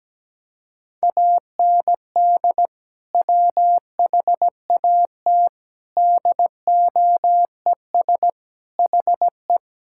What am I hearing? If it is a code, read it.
Morse code, heard as a keyed tone, about 17 wpm, 700 Hz. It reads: AND WHAT DOES HE
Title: Sentences from Top 1600 Words 17wpm